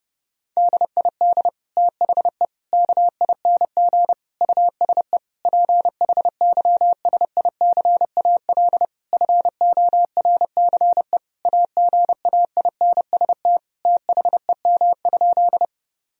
Morse code, heard as a keyed tone, about 30 wpm, 700 Hz. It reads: DID THE KING USE PHYSICAL FORCE AGAINST THEM?
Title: Sentences from Top 1000 Words - Review Part 4 - 30wpm